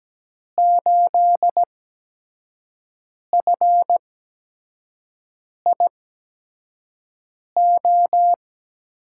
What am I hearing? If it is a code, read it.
Morse code, heard as a keyed tone, about 17 wpm, spaced as for 5 wpm, 700 Hz. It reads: 8FIO
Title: USA Call Signs - 6 Characters - 8x Character Spacing - 17wpm